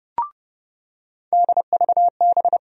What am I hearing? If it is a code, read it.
Morse code, heard as a keyed tone, about 32 wpm, 700 Hz. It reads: DVB
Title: USA Call Sign Postfixes - 3 Letters 30wpm